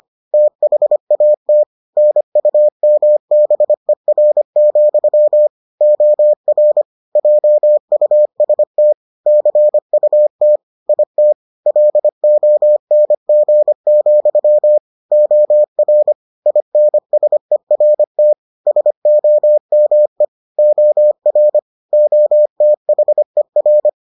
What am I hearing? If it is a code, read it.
Morse code, heard as a keyed tone, about 25 wpm, 600 Hz. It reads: THAT NUMBER, OR JUST CUT IT LONG, OR INSERT SOME OR OTHER